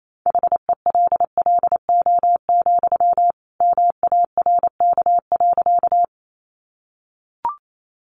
Morse code, heard as a keyed tone, about 28 wpm, 700 Hz. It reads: HELLO, MARK.
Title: Sentences from Top 1100 Words 28wpm